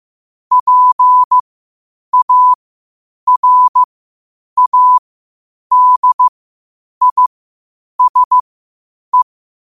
Morse code, heard as a keyed tone, about 15 wpm, spaced as for 8 wpm, 1000 Hz. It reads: PARADISE
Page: Quotes for Thu, 14 Aug 2025 in Morse Code at 8 words per minute.